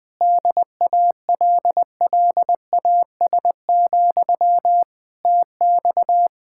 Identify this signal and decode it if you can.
Morse code, heard as a keyed tone, about 21 wpm, 700 Hz. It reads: DALLAS, TX
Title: Practice QSO - Exchange 1 - Beginning - 20wpm